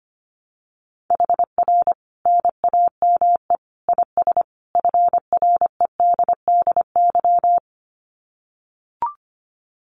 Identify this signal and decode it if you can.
Morse code, heard as a keyed tone, about 25 wpm, 700 Hz. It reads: HR NAME IS FREDDY
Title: Common QSO Phrases 25wpm